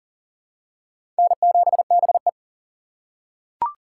Morse code, heard as a keyed tone, about 40 wpm, 700 Hz. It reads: N7BE
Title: USA Call Signs - 2x Character Spacing - 40wpm